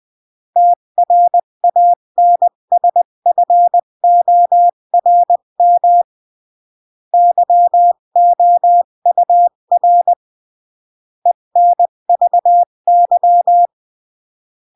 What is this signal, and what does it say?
Morse code, heard as a keyed tone, about 20 wpm, 700 Hz. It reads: TRANSFORM YOUR ENVY
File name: CW / jodorowsky / 82 commandments 20-15